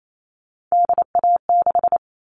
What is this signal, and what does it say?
Morse code, heard as a keyed tone, about 28 wpm, 700 Hz. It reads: DA6
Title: All Cut Numbers - 3-Digit 28wpm